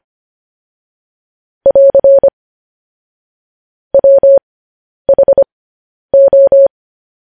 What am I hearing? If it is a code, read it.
Morse code, heard as a keyed tone, about 25 wpm, spaced as for 10 wpm, 550 Hz. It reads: <AR> WHO
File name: podcast / News headlines-1x-25x10